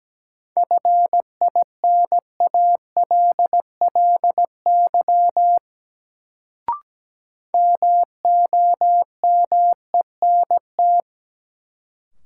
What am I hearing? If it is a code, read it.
Morse code, heard as a keyed tone, about 17 wpm, 700 Hz. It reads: FINALLY MOMENT
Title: Sentences from Top 1000 Words - New Word Drill - 17wpm